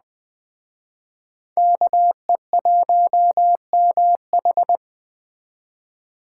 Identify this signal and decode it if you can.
Morse code, heard as a keyed tone, about 20 wpm, 700 Hz. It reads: KE1MH